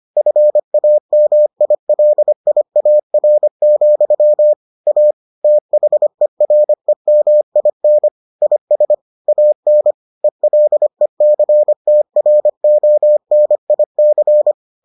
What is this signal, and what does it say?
Morse code, heard as a keyed tone, about 25 wpm, 600 Hz. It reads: FAMILIAR, A THEREMIN IS AN ELECTRONIC